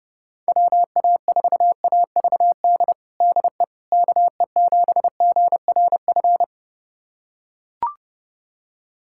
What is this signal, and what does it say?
Morse code, heard as a keyed tone, about 30 wpm, 700 Hz. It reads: WA4AVD DE KE7GRF
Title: USA Callsign DE Callsign 30wpm